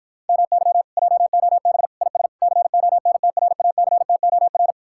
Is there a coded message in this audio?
Yes: Morse code from 0.3 to 4.7 s, heard as a keyed tone, about 59 wpm, 700 Hz. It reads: MY JOB IS CONTRACTOR